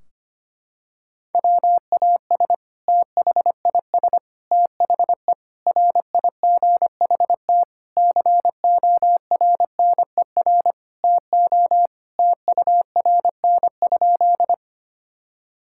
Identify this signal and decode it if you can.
Morse code, heard as a keyed tone, about 25 wpm, 700 Hz. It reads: WAS THIS THE RIGHT CORNER TO TURN?
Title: Sentences from Top 1600 Words 25wpm